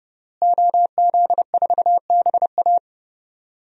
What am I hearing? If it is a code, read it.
Morse code, heard as a keyed tone, about 32 wpm, 700 Hz. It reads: OZ4BA
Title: DXCC-100 Call Signs - 5 Characters - Repeated 3x - 30wpm